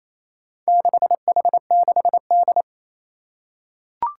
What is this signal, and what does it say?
Morse code, heard as a keyed tone, about 29 wpm, 700 Hz. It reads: 6H6D
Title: Four Challenging Letter-Number Combinations 28wpm